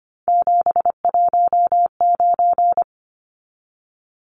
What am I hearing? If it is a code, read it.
Morse code, heard as a keyed tone, about 25 wpm, 690 Hz. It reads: 719
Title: Numbers - 3-Digit 25wpm